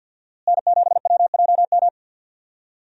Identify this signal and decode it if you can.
Morse code, heard as a keyed tone, about 52 wpm, 700 Hz. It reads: N7WJM
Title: USA Call Signs - 5 Characters - Repeated 3x - 50wpm